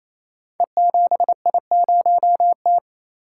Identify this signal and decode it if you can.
Morse code, heard as a keyed tone, about 28 wpm, 700 Hz. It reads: E7I0T
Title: Code Groups - 5 Letter:Numbers - 28wpm